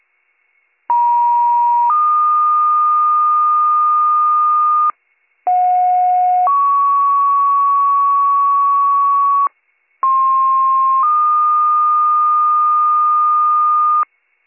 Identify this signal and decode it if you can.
Two-tone page, heard as a one-second tone, then a three-second tone, about 950 then 1220 Hz, about 730 then 1120 Hz, about 1060 then 1220 Hz. It is heard from 0.9 to 14.0 s.